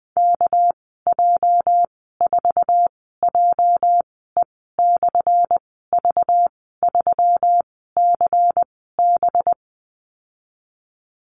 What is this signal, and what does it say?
Morse code, heard as a keyed tone, about 20 wpm, spaced as for 14 wpm, 700 Hz. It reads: KJ4JE/V3CB
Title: 219 Lesson 35g - Number 0 - Call Signs 20wpm